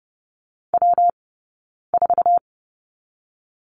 Morse code, heard as a keyed tone, about 30 wpm, 700 Hz. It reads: W 4
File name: Lesson 19d - Letter w - Two Characters 30wpm